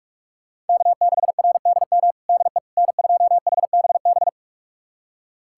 Morse code, heard as a keyed tone, about 47 wpm, 700 Hz. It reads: K6RDM DE N2HBB
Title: USA Callsign DE Callsign - 5 Characters - 45wpm